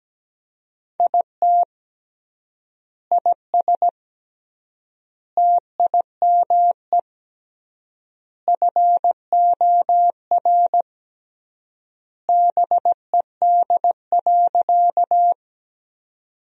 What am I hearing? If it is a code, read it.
Morse code, heard as a keyed tone, about 17 wpm, 700 Hz. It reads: IT IS TIME FOR BED.
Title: 3x Word-Spacing - Sentences from Top 500 Words - 17wpm